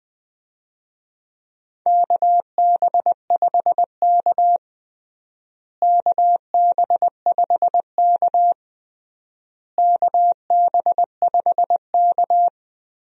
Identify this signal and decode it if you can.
Morse code, heard as a keyed tone, about 20 wpm, 700 Hz. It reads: KB5K KB5K KB5K
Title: USA Call Signs - 4 Characters - Repeated 3x - 20wpm